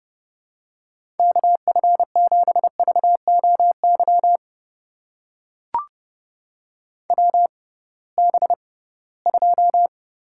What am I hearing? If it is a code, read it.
Morse code, heard as a keyed tone, about 30 wpm, spaced as for 19 wpm, 700 Hz. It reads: KF7VOY WB2
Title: USA Call Signs - 6 Characters - 6x Character Spacing - 30wpm